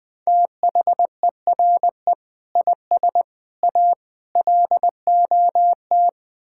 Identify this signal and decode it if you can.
Morse code, heard as a keyed tone, about 21 wpm, 700 Hz. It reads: THERE IS A LOT
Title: Build A Phrase - 5-Word Phrases - 20wpm